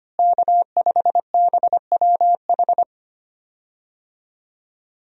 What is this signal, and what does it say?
Morse code, heard as a keyed tone, about 25 wpm, 700 Hz. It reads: K5BWH
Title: USA Call Signs - 5 Characters - Repeated 2x - 25wpm